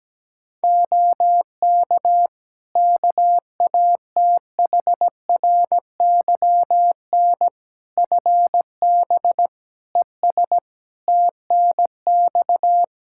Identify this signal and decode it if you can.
Morse code, heard as a keyed tone, about 17 wpm, 700 Hz. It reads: OK KATHRYN FB ES TNX
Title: Practice QSO - Exchange 2 - Info - 17wpm